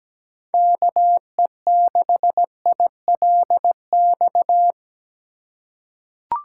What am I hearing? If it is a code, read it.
Morse code, heard as a keyed tone, about 17 wpm, 700 Hz. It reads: KE6ILX
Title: USA Call Signs - 6 Characters - 4x Character Spacing - 17wpm